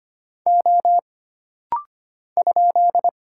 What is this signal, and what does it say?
Morse code, heard as a keyed tone, about 25 wpm, 700 Hz. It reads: O ?
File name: Single Character (Letter:Number:Punctuation:Prosign) - Rapid-Fire - 25wpm